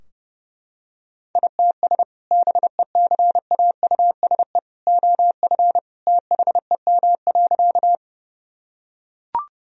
Morse code, heard as a keyed tone, about 32 wpm, 700 Hz. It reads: ITS BECAUSE OF THEM.
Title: Sentences from Top 1000 Words - Review Part 3 - 30wpm